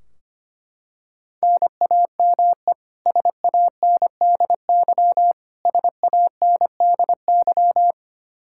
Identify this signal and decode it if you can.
Morse code, heard as a keyed tone, about 25 wpm, 700 Hz. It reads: NAME SANDY SANDY
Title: Practice QSO - Exchange 1 - Beginning - 25wpm